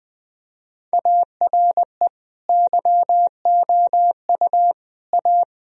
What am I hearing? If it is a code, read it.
Morse code, heard as a keyed tone, about 20 wpm, 700 Hz. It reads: ARE YOU A
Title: Sentences from Top 1000 Words - Review Part 4 - 20wpm